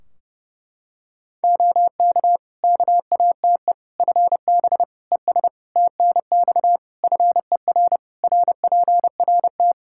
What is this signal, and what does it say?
Morse code, heard as a keyed tone, about 32 wpm, 700 Hz. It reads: OK KATE FB ES TNX FER RPRT
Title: Practice QSO - Exchange 2 - Info - 30wpm